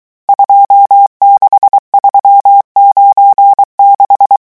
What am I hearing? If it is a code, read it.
Morse code, heard as a keyed tone, about 23 wpm, 800 Hz. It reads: 26396
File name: morse mp3 / revision11 chiffres vitesse 22